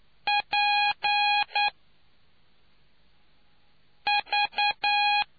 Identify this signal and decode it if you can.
Morse code, heard as a keyed tone, about 9 wpm, 820 Hz. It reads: P V